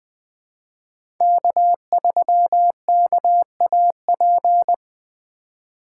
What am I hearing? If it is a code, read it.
Morse code, heard as a keyed tone, about 20 wpm, 700 Hz. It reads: K3KAP
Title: USA Call Signs - 5 Characters - 4x Character Spacing - 20wpm